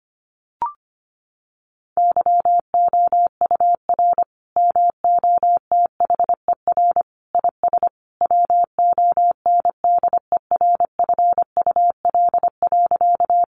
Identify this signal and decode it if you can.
Morse code, heard as a keyed tone, about 25 wpm, 700 Hz. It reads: YOUR MOTHER IS WONDERFUL.
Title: Sentences from Top 1100 Words 25wpm